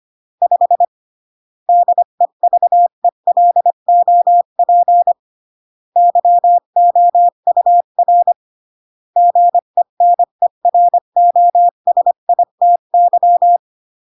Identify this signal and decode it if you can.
Morse code, heard as a keyed tone, about 25 wpm, 700 Hz. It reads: 5 DEVELOP YOUR GENEROSITY